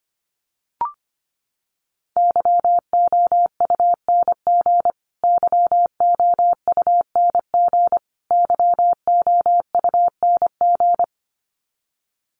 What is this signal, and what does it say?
Morse code, heard as a keyed tone, about 25 wpm, 700 Hz. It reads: YOUNG YOUNG YOUNG
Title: Top 500 Words Repeated 25wpm